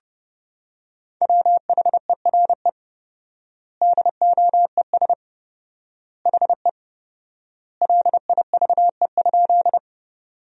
Morse code, heard as a keyed tone, about 30 wpm, 700 Hz. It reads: WHERE DOES HE LIVE?
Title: 4x Word-Spacing - Sentences from Top 200 Words - 30wpm